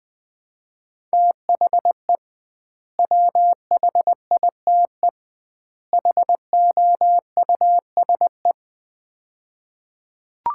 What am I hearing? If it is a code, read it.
Morse code, heard as a keyed tone, about 20 wpm, 700 Hz. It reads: THE WHITE HOUSE
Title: 2x Word-Spacing - Sets of 3 Words - 20wpm